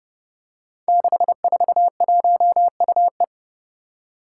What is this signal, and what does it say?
Morse code, heard as a keyed tone, about 30 wpm, 700 Hz. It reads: 641UE
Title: Five Challenging Letter-Number Combinations 30wpm